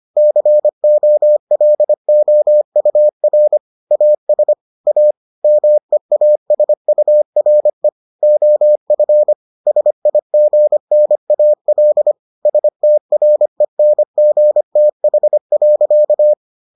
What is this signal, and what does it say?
Morse code, heard as a keyed tone, about 25 wpm, 600 Hz. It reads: COLOUR AS A MEASURE OF SIGNAL STRENGTH.